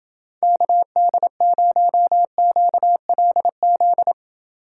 Morse code, heard as a keyed tone, about 29 wpm, 700 Hz. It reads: KD0QLZ